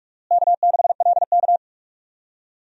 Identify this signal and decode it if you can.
Morse code, heard as a keyed tone, about 43 wpm, 700 Hz. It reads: K6LK
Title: USA Call Signs - 4 Characters - Repeated 3x - 45wpm